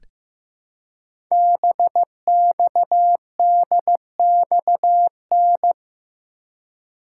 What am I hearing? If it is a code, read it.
Morse code, heard as a keyed tone, about 15 wpm, 700 Hz. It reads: BXDXN
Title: Code Groups - 5 Letter:Numbers - 15wpm